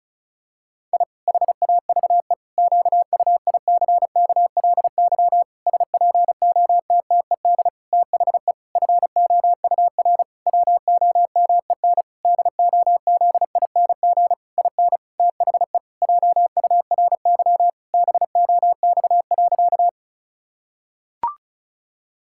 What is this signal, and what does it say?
Morse code, heard as a keyed tone, about 34 wpm, 700 Hz. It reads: I HAVE QUICKLY SPOTTED THE FOUR WOMEN DOZING IN THE JURY BOX.
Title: Pangram Challenge v2 - 35wpm